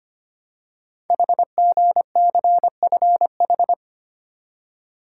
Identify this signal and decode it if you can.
Morse code, heard as a keyed tone, about 25 wpm, 700 Hz. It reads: HGCFH
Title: Code Groups - 5 Letters - 25wpm